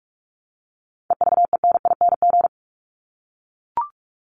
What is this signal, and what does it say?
Morse code, heard as a keyed tone, about 44 wpm, 700 Hz. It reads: EVENING
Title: Top 1501-1750 Words 45wpm